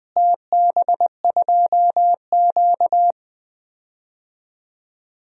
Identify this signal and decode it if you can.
Morse code, heard as a keyed tone, about 21 wpm, 700 Hz. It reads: TB2Q
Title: DXCC-100 Call Signs - 4 Characters - Repeated 3x - 20wpm